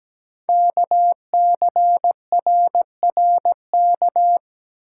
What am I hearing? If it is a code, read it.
Morse code, heard as a keyed tone, about 17 wpm, 700 Hz. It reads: KCRRK
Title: Code Groups - 5 Letter:Numbers - 17wpm